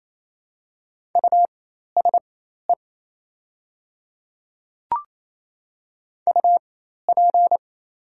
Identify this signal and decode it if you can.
Morse code, heard as a keyed tone, about 28 wpm, spaced as for 13 wpm, 700 Hz. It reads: USE UP
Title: 4x Character Spacing - Top 100 Words - 28wpm